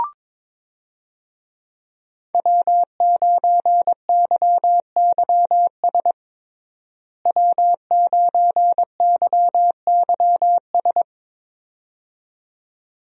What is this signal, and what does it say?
Morse code, heard as a keyed tone, about 22 wpm, 700 Hz. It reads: W9YYS W9YYS
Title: USA Call Signs - 5 Characters - Repeated 2x - 22wpm